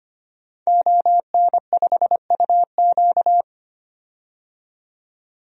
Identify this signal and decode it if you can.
Morse code, heard as a keyed tone, about 25 wpm, 700 Hz. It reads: ON5UQ
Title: DXCC-100 Call Signs - 5 Characters - 25wpm